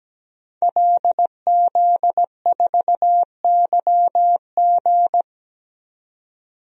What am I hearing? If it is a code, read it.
Morse code, heard as a keyed tone, about 17 wpm, 700 Hz. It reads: LZ4YG
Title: DXCC-100 Call Signs - 5 Characters - 17wpm